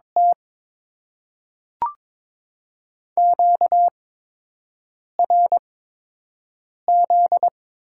Morse code, heard as a keyed tone, about 22 wpm, spaced as for 6 wpm, 700 Hz. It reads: T QRZ
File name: Top 90 QSO Elements - 8x Character Spacing - 22wpm